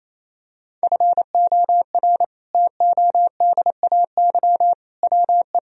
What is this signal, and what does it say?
Morse code, heard as a keyed tone, about 28 wpm, 700 Hz. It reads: FOR TODAY WE
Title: Sentences from Top 1000 Words - Review Part 1 - 28wpm